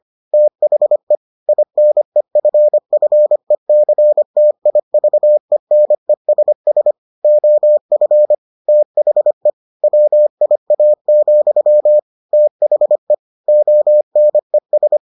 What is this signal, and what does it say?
Morse code, heard as a keyed tone, about 25 wpm, 600 Hz. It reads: THE INEFFECTIVENESS OF THE WIA, THE ONES